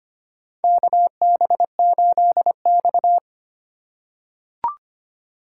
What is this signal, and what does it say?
Morse code, heard as a keyed tone, about 25 wpm, 700 Hz. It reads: KB8X
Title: USA Call Signs - 4x Character Spacing - 25wpm